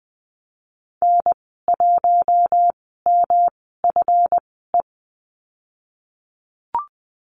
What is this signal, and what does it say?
Morse code, heard as a keyed tone, about 20 wpm, spaced as for 15 wpm, 700 Hz. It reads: N1MFE
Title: 140 Lesson 23g - Letter f - Call Signs 20wpm